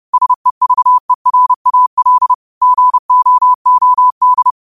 Morse code, heard as a keyed tone, about 30 wpm, 1000 Hz. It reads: SEVERAL GOOD
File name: qotdfiles / 30wpm